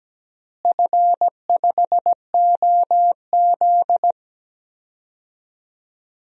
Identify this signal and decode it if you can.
Morse code, heard as a keyed tone, about 17 wpm, 700 Hz. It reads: F5OZ
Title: DXCC-100 Call Signs - 4 Characters - Repeated 2x - 17wpm